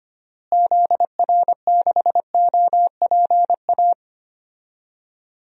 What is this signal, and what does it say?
Morse code, heard as a keyed tone, about 25 wpm, 700 Hz. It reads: ZR6OPA
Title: DXCC-100 Call Signs - 6 Characters - Repeated 2x - 25wpm